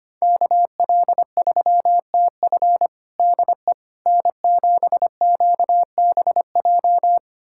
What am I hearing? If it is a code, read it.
Morse code, heard as a keyed tone, about 25 wpm, 700 Hz. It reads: KL3TF DE N7QBJ
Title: USA Callsign DE Callsign - 5 Characters - 25wpm